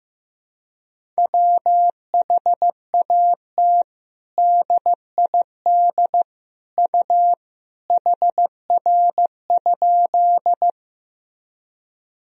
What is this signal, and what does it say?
Morse code, heard as a keyed tone, about 16 wpm, 700 Hz. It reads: WHAT DID U HR?
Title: Sentences from Top 300 Words and Abbreviations 15wpm